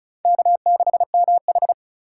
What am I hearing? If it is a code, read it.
Morse code, heard as a keyed tone, about 36 wpm, 700 Hz. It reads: K6MH
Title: USA Call Signs - 4 Characters - 2x Character Spacing - 35wpm